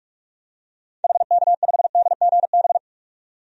Morse code, heard as a keyed tone, about 41 wpm, 700 Hz. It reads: HK5DGB
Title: DXCC-100 Call Signs - 6 Characters - 45wpm